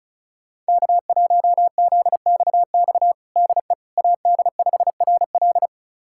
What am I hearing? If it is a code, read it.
Morse code, heard as a keyed tone, about 36 wpm, 700 Hz. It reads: K1ZXX DE AD5RL
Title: USA Callsign DE Callsign - 5 Characters - 35wpm